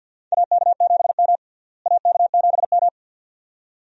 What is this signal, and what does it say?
Morse code, heard as a keyed tone, about 46 wpm, 700 Hz. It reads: AK7M AK7M
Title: USA Call Signs - 4 Characters - Repeated 2x - 50wpm